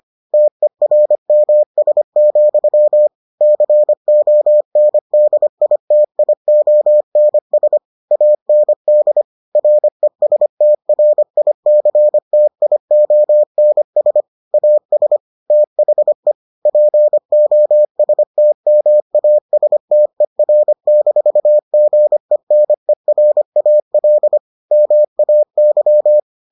Morse code, heard as a keyed tone, about 25 wpm, 600 Hz. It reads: TERMS, CONDITIONS AND RESTRICTIONS AS THE POSTMASTER-GENERAL MAY